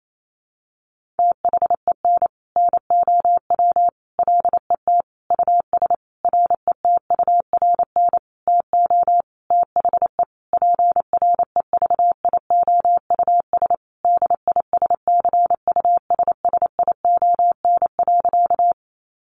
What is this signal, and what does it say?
Morse code, heard as a keyed tone, about 28 wpm, 700 Hz. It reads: THEN NOW LET US RETURN TO THE PREVIOUS DISCUSSION.
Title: Sentences from Top 1500 Words 28wpm